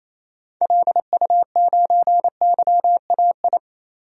Morse code, heard as a keyed tone, about 28 wpm, 700 Hz. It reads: LU9YAI
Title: DXCC-100 Call Signs - 6 Characters - 28wpm